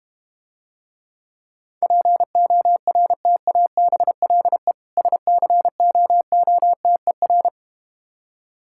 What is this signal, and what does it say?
Morse code, heard as a keyed tone, about 32 wpm, 700 Hz. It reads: PORTABLE SCOOTER